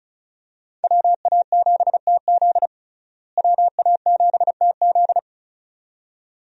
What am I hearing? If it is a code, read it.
Morse code, heard as a keyed tone, about 36 wpm, 700 Hz. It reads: WA7TZ WA7TZ
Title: USA Call Signs - 5 Characters - Repeated 2x - 35wpm